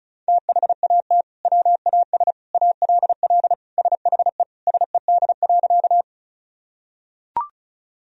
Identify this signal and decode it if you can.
Morse code, heard as a keyed tone, about 36 wpm, 700 Hz. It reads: THAT WAS ALL SHE SED.
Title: Sentences from Top 100 Words and Abbreviations 35wpm